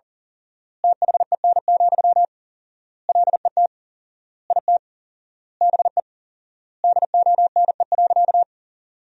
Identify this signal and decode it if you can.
Morse code, heard as a keyed tone, about 40 wpm, 700 Hz. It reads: THEN, LET IT BE DONE.
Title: 4x Word-Spacing - Sentences from Top 500 Words - 40wpm